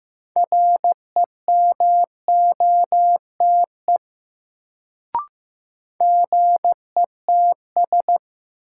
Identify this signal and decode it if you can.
Morse code, heard as a keyed tone, about 15 wpm, 700 Hz. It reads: REMOTE GETS
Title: Sentences from Top 1300 Words - New Word Drill - 15wpm